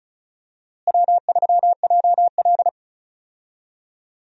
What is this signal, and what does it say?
Morse code, heard as a keyed tone, about 36 wpm, 700 Hz. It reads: W3JL
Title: USA Call Signs - 4 Characters - Repeated 3x - 35wpm